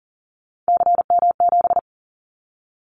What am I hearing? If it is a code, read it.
Morse code, heard as a keyed tone, about 40 wpm, 700 Hz. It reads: CM7
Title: Callsign Prefixes for Top 100 DXCC - 40wpm